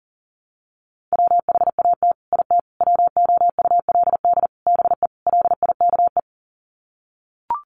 Morse code, heard as a keyed tone, about 40 wpm, 700 Hz. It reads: WHAT IT WOULD BE LIKE
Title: Sets of 5 Words Encore 40wpm